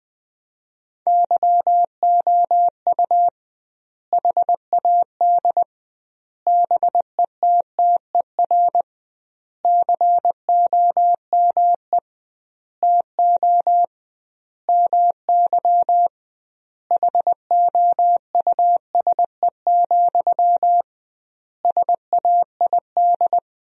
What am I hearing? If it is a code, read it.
Morse code, heard as a keyed tone, about 20 wpm, 700 Hz. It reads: YOU HAD BETTER COME TO MY HOUSE, SAID
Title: 2x Word-Spacing - Sentences from Top 400 Words - 20wpm